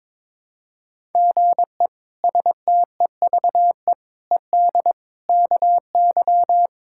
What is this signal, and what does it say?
Morse code, heard as a keyed tone, about 22 wpm, 700 Hz. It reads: GE STEVE ED KY
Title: K1USN SST QSO Practice - 22wpm